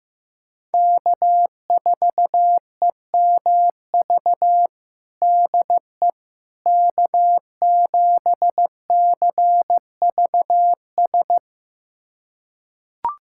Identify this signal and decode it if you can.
Morse code, heard as a keyed tone, about 15 wpm, 700 Hz. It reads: K4EMV DE K7CVS
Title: USA Callsign DE Callsign - 5 Characters - 15wpm